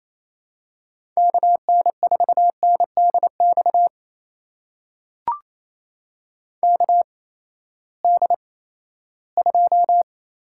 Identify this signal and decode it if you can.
Morse code, heard as a keyed tone, about 28 wpm, spaced as for 16 wpm, 700 Hz. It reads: KN4NDX KD2
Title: USA Call Signs - 6 Characters - 8x Character Spacing - 28wpm